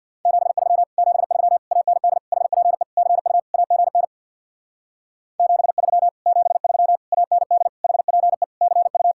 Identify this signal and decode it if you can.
Morse code, heard as a keyed tone, about 67 wpm, 700 Hz. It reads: 73 73 AND HPE CU AGN 73 73 AND HPE CU